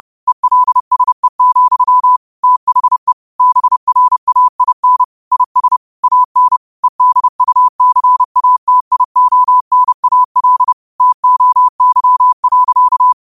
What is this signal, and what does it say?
Morse code, heard as a keyed tone, about 30 wpm, 1000 Hz. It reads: ELSE, THE BRAIN IS AN EDUCATIONAL TOY.